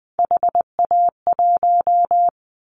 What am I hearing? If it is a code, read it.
Morse code, heard as a keyed tone, about 20 wpm, 700 Hz. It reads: HA1
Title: Callsign Prefixes for Top 50 DXCC - 20wpm